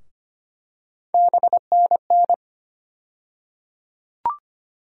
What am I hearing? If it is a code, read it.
Morse code, heard as a keyed tone, about 25 wpm, 700 Hz. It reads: BNN
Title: USA Call Sign Postfixes - 3 Letters 25wpm